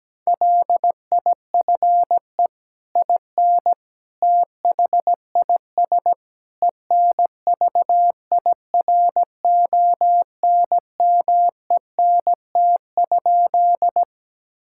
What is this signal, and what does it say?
Morse code, heard as a keyed tone, about 17 wpm, 700 Hz. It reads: LIFE IN THIS ENVIRONMENT?
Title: Sentences from Top 1000 Words - Review Part 5 - 17wpm